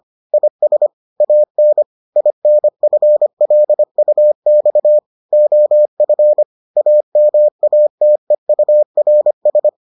Morse code, heard as a keyed tone, about 25 wpm, 600 Hz. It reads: IS AN INFLUX OF AMATEURS